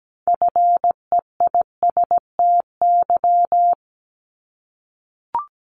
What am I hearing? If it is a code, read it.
Morse code, heard as a keyed tone, about 17 wpm, 700 Hz. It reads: FEISTY
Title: Multi-letter Phonemes 17wpm